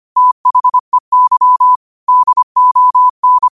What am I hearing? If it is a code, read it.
Morse code, heard as a keyed tone, about 25 wpm, 1000 Hz. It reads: THEY DON
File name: qotdfiles / 25wpm